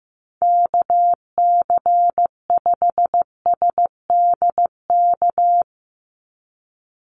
Morse code, heard as a keyed tone, about 15 wpm, 700 Hz. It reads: KC5SDK
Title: USA Call Signs 15wpm